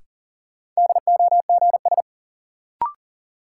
Morse code, heard as a keyed tone, about 38 wpm, 700 Hz. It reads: DOGS
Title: Sentences from Top 1600 Words - New Word Drill Alt - 40wpm